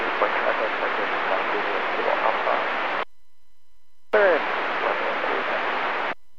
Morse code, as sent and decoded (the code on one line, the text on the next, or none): none